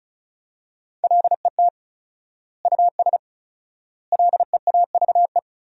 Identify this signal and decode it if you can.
Morse code, heard as a keyed tone, about 36 wpm, 700 Hz. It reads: LET US LEAVE
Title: 4x Word-Spacing - Sentences from Top 400 Words - 35wpm